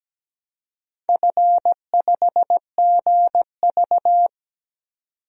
Morse code, heard as a keyed tone, about 17 wpm, 700 Hz. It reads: F5GV
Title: DXCC-100 Call Signs - 4 Characters - 17wpm